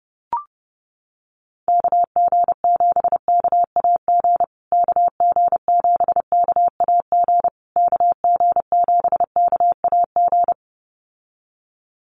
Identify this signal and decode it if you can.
Morse code, heard as a keyed tone, about 28 wpm, 700 Hz. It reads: KG7KAG KG7KAG KG7KAG
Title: USA Call Signs Repeated 3x 30wpm